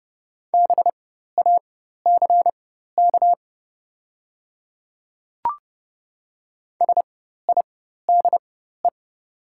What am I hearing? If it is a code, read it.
Morse code, heard as a keyed tone, about 30 wpm, spaced as for 14 wpm, 700 Hz. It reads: BACK SIDE
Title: 4x Character Spacing - Top 200 Words - 30wpm